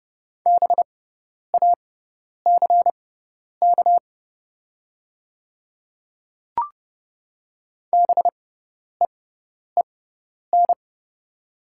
Morse code, heard as a keyed tone, about 30 wpm, spaced as for 10 wpm, 700 Hz. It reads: BACK BEEN
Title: 6x Character Spacing - Top 200 Words - 30wpm